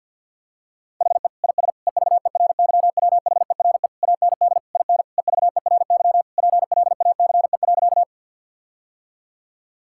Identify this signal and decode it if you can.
Morse code, heard as a keyed tone, about 50 wpm, 700 Hz. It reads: HE IS EVERYWHERE AND IN EVERY PLACE.
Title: Sentences from Top 1600 Words 50wpm